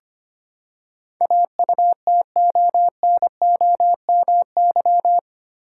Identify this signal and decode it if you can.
Morse code, heard as a keyed tone, about 25 wpm, 700 Hz. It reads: AUTONOMY
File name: Eight Letter Words - 25wpm